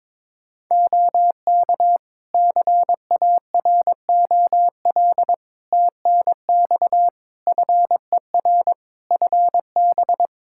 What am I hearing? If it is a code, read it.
Morse code, heard as a keyed tone, about 22 wpm, 700 Hz. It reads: OK CAROL TNX FER FB
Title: Practice QSO - Exchange 4 - Ending - 22wpm